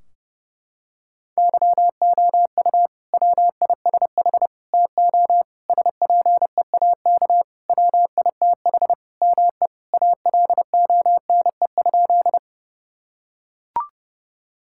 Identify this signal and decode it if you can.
Morse code, heard as a keyed tone, about 30 wpm, 700 Hz. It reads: YOU WISH TO SPEAK WITH ME ALONE?
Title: Sentences from Top 1600 Words 30wpm